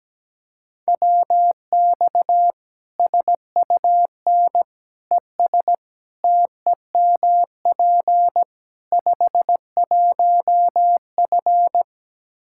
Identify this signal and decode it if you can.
Morse code, heard as a keyed tone, about 17 wpm, 700 Hz. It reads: WX SUN ES TEMP 51F
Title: Practice QSO - Exchange 2 - Info - 17wpm